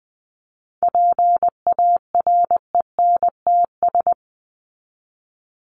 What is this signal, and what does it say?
Morse code, heard as a keyed tone, about 20 wpm, 700 Hz. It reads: PARENTS
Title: ES:S Suffix - Top 100 Words with Base Words 20wpm